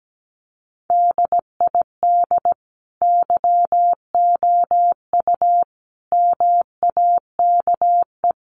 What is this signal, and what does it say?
Morse code, heard as a keyed tone, about 17 wpm, 700 Hz. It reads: DID YOU MAKE
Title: Sentences from Top 1300 Words 17wpm